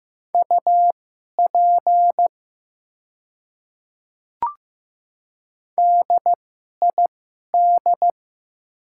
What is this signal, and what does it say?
Morse code, heard as a keyed tone, about 15 wpm, spaced as for 11 wpm, 700 Hz. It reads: UP DID
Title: 2x Character Spacing - Top 100 Words - 15wpm